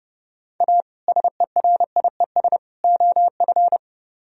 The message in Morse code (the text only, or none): A SERIES OF